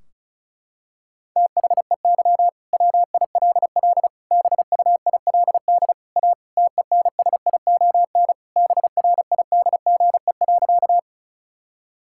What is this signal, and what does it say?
Morse code, heard as a keyed tone, about 36 wpm, 700 Hz. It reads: THEY WILL BUILD A TENSION BRIDGE.
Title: Sentences from Top 1600 Words 35wpm